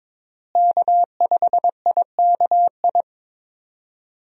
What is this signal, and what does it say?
Morse code, heard as a keyed tone, about 23 wpm, 700 Hz. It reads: K5IKI
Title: USA Call Signs - 5 Characters - Repeated 3x - 22wpm